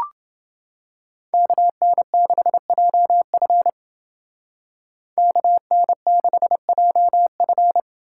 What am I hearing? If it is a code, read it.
Morse code, heard as a keyed tone, about 28 wpm, 700 Hz. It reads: KN6JF KN6JF